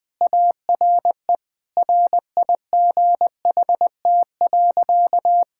Morse code, heard as a keyed tone, about 20 wpm, 700 Hz. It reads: ARE RIGHT.
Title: Sentences with 3-Word Phrases - Spoken Beginning - 20wpm